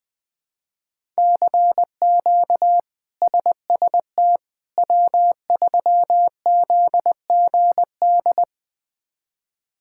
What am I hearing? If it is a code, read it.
Morse code, heard as a keyed tone, about 20 wpm, 700 Hz. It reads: CQ SST W3ZGD
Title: K1USN SST QSO Practice - 20wpm